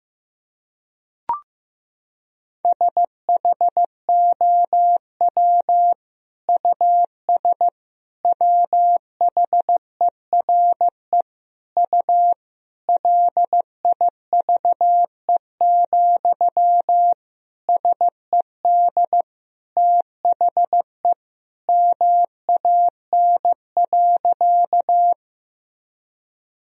Morse code, heard as a keyed tone, about 15 wpm, 700 Hz. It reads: SHOW US WHERE U LIVE, SED THE MAN.
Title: Sentences from Top 200 Words and Abbreviations 15wpm